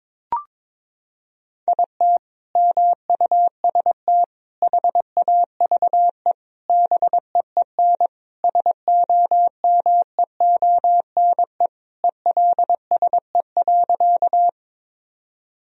Morse code, heard as a keyed tone, about 21 wpm, 700 Hz. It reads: IT MUST HAVE BEEN SOMEONE ELSE.
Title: Sentences from Top 1000 Words - Review Part 4 - 22wpm